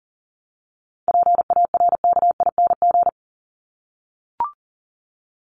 Morse code, heard as a keyed tone, about 40 wpm, 700 Hz. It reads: PARKING
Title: Top 1501-1750 Words 40wpm